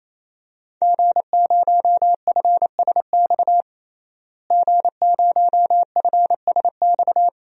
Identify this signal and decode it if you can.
Morse code, heard as a keyed tone, about 28 wpm, 700 Hz. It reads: G0FSX G0FSX
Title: DXCC-100 Call Signs - 5 Characters - Repeated 3x - 28wpm